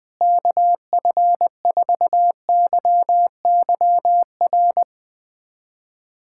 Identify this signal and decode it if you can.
Morse code, heard as a keyed tone, about 20 wpm, 700 Hz. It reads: KF4YYR
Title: USA Callsign DE Callsign 20wpm